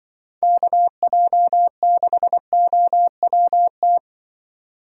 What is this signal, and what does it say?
Morse code, heard as a keyed tone, about 22 wpm, 700 Hz. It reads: KJ6OWT
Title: Speed Racing - USA 2x3 Call Signs 20wpm